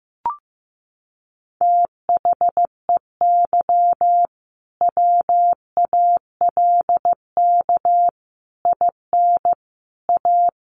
Morse code, heard as a keyed tone, about 15 wpm, 700 Hz. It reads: THEY WALK IN A
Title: Sentences from Top 1500 Words 15wpm